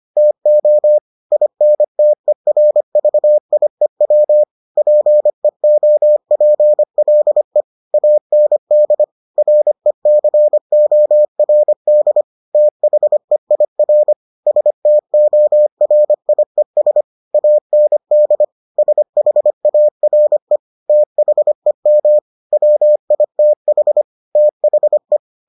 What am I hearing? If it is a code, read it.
Morse code, heard as a keyed tone, about 25 wpm, 600 Hz. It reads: TO INTERVIEW PEOPLE AND RECORD THEIR STORIES AND SHARE THEM WITH THE